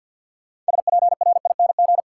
Morse code, heard as a keyed tone, about 52 wpm, 700 Hz. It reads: SPRING
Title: Sentences from Top 1200 Words - New Word Drill Alt - 50wpm